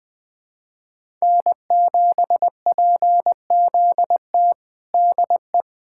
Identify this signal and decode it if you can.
Morse code, heard as a keyed tone, about 20 wpm, 700 Hz. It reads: N7PZT DE
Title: USA Callsign DE Callsign 20wpm